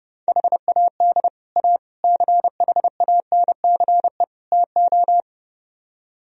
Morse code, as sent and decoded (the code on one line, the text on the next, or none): .... .- -.. / .- / -.-. .... .- -. -.-. / - ---
HAD A CHANCE TO